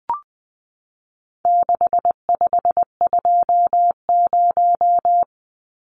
Morse code, heard as a keyed tone, about 20 wpm, 700 Hz. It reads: 6520
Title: Numbers - 4-Digit 20wpm